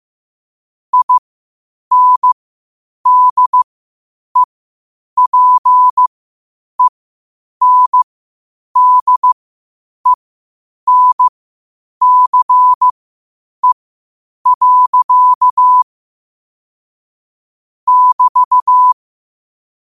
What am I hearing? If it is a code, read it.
Morse code, heard as a keyed tone, about 15 wpm, spaced as for 8 wpm, 1000 Hz. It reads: INDEPENDENCE. <BT>
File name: qotdfiles / 8wpm